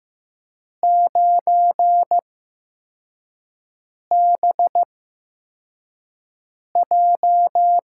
Morse code, heard as a keyed tone, about 15 wpm, spaced as for 4 wpm, 700 Hz. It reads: 9BJ
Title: USA Call Signs - 8x Character Spacing - 15wpm